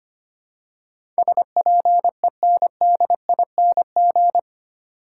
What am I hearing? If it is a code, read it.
Morse code, heard as a keyed tone, about 25 wpm, 700 Hz. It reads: SPENDING
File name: Eight Letter Words - 25wpm